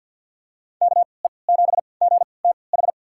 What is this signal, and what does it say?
Morse code, heard as a keyed tone, about 47 wpm, spaced as for 28 wpm, 700 Hz. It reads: KE7GTH
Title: USA Call Signs - 6 Characters - 3x Character Spacing - 50wpm